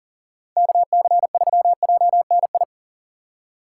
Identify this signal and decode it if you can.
Morse code, heard as a keyed tone, about 52 wpm, 700 Hz. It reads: KC3JNI